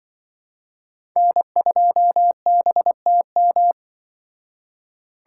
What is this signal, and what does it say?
Morse code, heard as a keyed tone, about 24 wpm, 700 Hz. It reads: N2BTM
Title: Speed Racing - USA Call Signs 20wpm